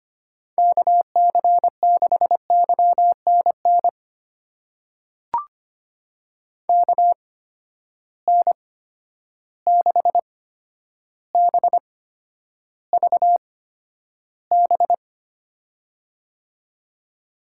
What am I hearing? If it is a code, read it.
Morse code, heard as a keyed tone, about 25 wpm, spaced as for 11 wpm, 700 Hz. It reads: KC6YNN KN6BVB
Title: USA Call Signs - 6 Characters - 8x Character Spacing - 25wpm